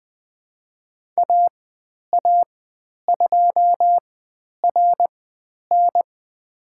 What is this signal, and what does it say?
Morse code, heard as a keyed tone, about 20 wpm, spaced as for 10 wpm, 700 Hz. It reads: AA2RN
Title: 109 Lesson 18g - Letter m - Call Signs 20wpm